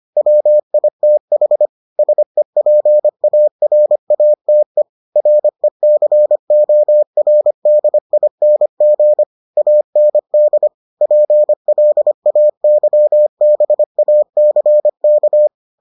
Morse code, heard as a keyed tone, about 25 wpm, 600 Hz. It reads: WITH SEPARATE RECORDING AND PLAYBACK